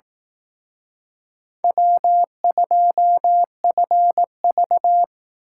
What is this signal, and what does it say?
Morse code, heard as a keyed tone, about 18 wpm, 700 Hz. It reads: W2FV
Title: Speed Racing - USA Call Signs 15wpm